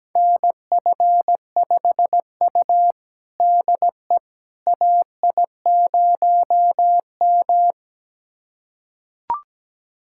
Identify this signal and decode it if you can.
Morse code, heard as a keyed tone, about 17 wpm, 700 Hz. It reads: NF5U DE AI0M
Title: USA Callsign DE Callsign - 4 Characters - 17wpm